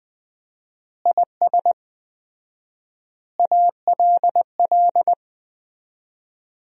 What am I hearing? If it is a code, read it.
Morse code, heard as a keyed tone, about 21 wpm, 700 Hz. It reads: IS ALL
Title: 4x Word-Spacing - Sentences from Top 100 Words - 20wpm